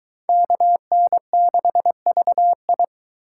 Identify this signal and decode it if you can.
Morse code, heard as a keyed tone, about 22 wpm, 700 Hz. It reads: KN6VI